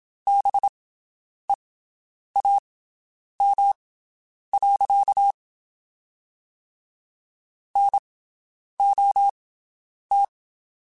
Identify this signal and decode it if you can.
Morse code, heard as a keyed tone, about 27 wpm, spaced as for 9 wpm, 800 Hz. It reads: BEAM. NOT
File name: (spaced) / QSOs Spaced-mp3s / WD0BGP 1 @25 sp